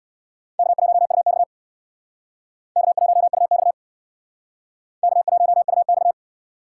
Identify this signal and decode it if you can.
Morse code, heard as a keyed tone, about 60 wpm, 700 Hz. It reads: K1UX K1UX K1UX
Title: Speed Racing - USA 1x2 Call Signs 50wpm